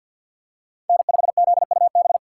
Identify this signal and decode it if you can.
Morse code, heard as a keyed tone, about 52 wpm, 700 Hz. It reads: N5ZUB
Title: USA Call Signs - 2x Character Spacing - 50wpm